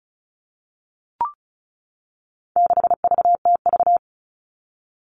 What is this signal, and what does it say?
Morse code, heard as a keyed tone, about 34 wpm, 700 Hz. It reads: 6VTV
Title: All Cut Numbers - 4-Digit 35wpm